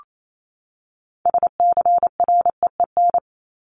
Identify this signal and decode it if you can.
Morse code, heard as a keyed tone, about 28 wpm, 700 Hz. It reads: SCREEN
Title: Top 751-1000 Words 28wpm